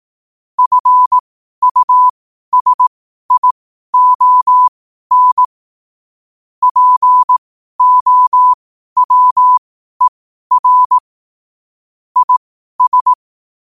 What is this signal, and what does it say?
Morse code, heard as a keyed tone, about 18 wpm, spaced as for 12 wpm, 1000 Hz. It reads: FUSION POWER IS
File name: qotdfiles / 12wpm